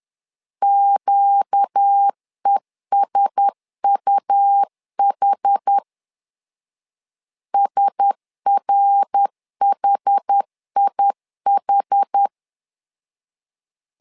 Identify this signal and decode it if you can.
Morse code, heard as a keyed tone, about 10 wpm, 800 Hz. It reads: QESUH SRHIH
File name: LEZIONI / 4 Q1S 10wpm 2133